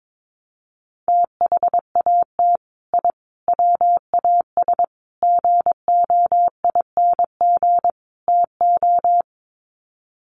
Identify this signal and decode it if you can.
Morse code, heard as a keyed tone, about 22 wpm, 700 Hz. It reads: THAT I WAS GOING TO
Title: Sets of 5 Words Encore 22wpm